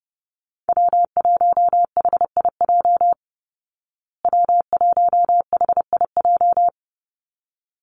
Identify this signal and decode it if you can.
Morse code, heard as a keyed tone, about 28 wpm, 700 Hz. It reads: W1HIJ W1HIJ
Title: Call Signs Repeated 3x - 30wpm